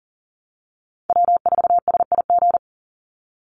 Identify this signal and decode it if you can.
Morse code, heard as a keyed tone, about 39 wpm, 700 Hz. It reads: W4SIG
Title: Call Signs Repeated 3x - 40wpm